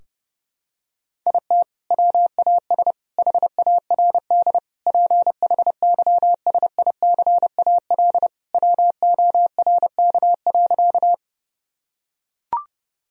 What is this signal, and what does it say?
Morse code, heard as a keyed tone, about 30 wpm, 700 Hz. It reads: IT WAS HARD PHYSICAL WORK.
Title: Sentences from Top 1000 Words - Review Part 4 - 30wpm